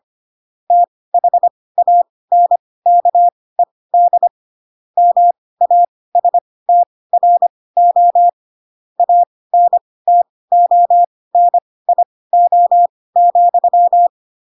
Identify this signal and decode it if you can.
Morse code, heard as a keyed tone, about 25 wpm, spaced as for 18 wpm, 700 Hz. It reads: THANKED MASTRO ANTONIO,
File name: Pinocchio / Pinocchio ch2 12 25 ik1hge